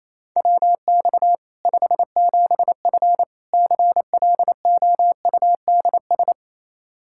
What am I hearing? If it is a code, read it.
Morse code, heard as a keyed tone, about 28 wpm, 700 Hz. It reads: WX 57F CLOUDS
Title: K1USN SST QSO Practice 28wpm